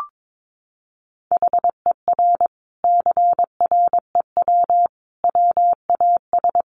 Morse code, heard as a keyed tone, about 22 wpm, 700 Hz. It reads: HER CREW WAS
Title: Sentences from Top 1500 Words 22wpm